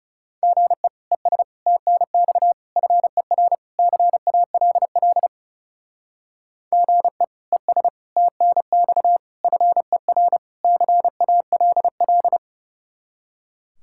Morse code, heard as a keyed tone, about 36 wpm, 700 Hz. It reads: GE ES TNX FER CALL GE ES TNX FER CALL
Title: Speed-Racing Common QSO Phrases - 30wpm